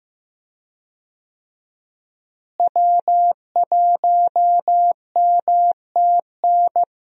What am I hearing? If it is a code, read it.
Morse code, heard as a keyed tone, about 16 wpm, 700 Hz. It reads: W1MTN
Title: USA Call Signs - 5 Characters - Repeated 3x - 15wpm